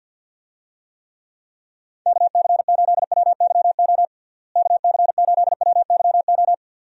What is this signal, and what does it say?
Morse code, heard as a keyed tone, about 50 wpm, 700 Hz. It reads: KC8WYO KC8WYO
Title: USA Call Signs - 6 Characters - Repeated 2x - 50wpm